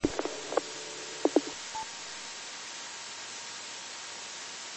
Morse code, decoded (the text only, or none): none